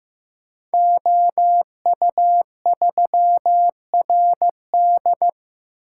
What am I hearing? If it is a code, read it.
Morse code, heard as a keyed tone, about 15 wpm, 700 Hz. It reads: OU3RD
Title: Code Groups - 5 Letter:Numbers - 15wpm